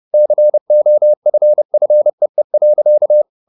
Morse code, heard as a keyed tone, about 30 wpm, 600 Hz. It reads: COFFEE.